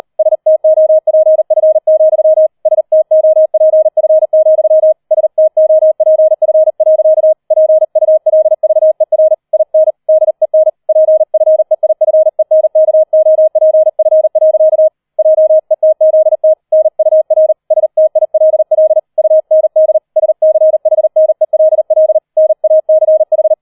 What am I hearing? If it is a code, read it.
Morse code, heard as a keyed tone, about 38 wpm, 630 Hz. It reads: STOPF, STOPF, STOPF. PULVER IN DEN PFEIFENKOPF. JETZT NUR STILL UND SCHNELL NACH
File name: 004 Vierter Streich 40wpm